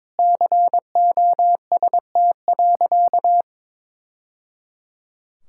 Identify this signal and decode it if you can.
Morse code, heard as a keyed tone, about 22 wpm, 700 Hz. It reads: COST.
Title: Sentences from Top 1000 Words - Review Part 6 - 22wpm